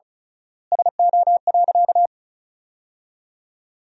Morse code, heard as a keyed tone, about 36 wpm, 700 Hz. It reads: SO.
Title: 3x Word-Spacing - Sentences from Top 300 Words - 35wpm